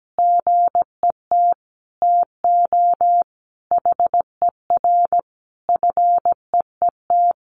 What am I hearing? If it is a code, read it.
Morse code, heard as a keyed tone, about 17 wpm, 700 Hz. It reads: GET TO HER FEET
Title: Sentences from Top 1300 Words 17wpm